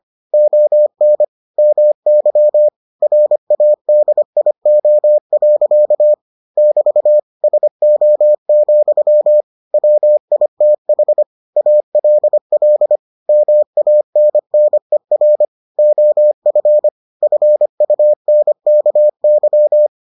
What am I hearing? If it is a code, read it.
Morse code, heard as a keyed tone, about 25 wpm, 600 Hz. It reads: ON MY RADIO. <BT> SO, WITH ALL MANNER OF FUNKY